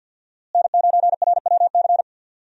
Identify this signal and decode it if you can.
Morse code, heard as a keyed tone, about 46 wpm, 700 Hz. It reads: N9RWC
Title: USA Call Signs - 5 Characters - Repeated 3x - 50wpm